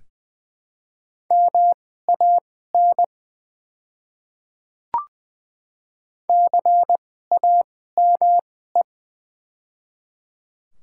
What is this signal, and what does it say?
Morse code, heard as a keyed tone, about 20 wpm, spaced as for 14 wpm, 700 Hz. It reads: MAN CAME
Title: 2x Character Spacing - Top 200 Words - 20wpm